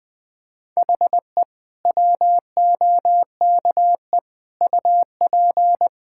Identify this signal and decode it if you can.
Morse code, heard as a keyed tone, about 20 wpm, 700 Hz. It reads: HE WOKE UP
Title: Sentences with 6-Word Phrases - 20wpm